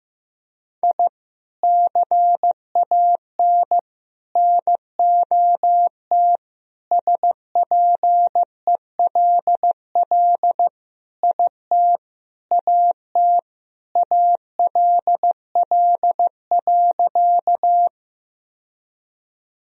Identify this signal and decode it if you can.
Morse code, heard as a keyed tone, about 16 wpm, 700 Hz. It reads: I CAN NOT SPELL IT AT ALL.
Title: Sentences from Top 1000 Words - Review Part 1 - 15wpm